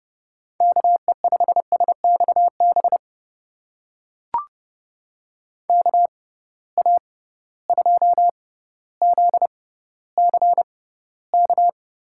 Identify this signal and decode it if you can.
Morse code, heard as a keyed tone, about 30 wpm, spaced as for 15 wpm, 700 Hz. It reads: KE5SXB KA2ZCK
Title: USA Call Signs - 6 Characters - 6x Character Spacing - 30wpm